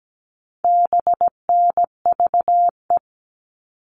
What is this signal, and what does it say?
Morse code, heard as a keyed tone, about 17 wpm, 700 Hz. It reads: BNVE
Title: All Cut Numbers - 4-Digit 17wpm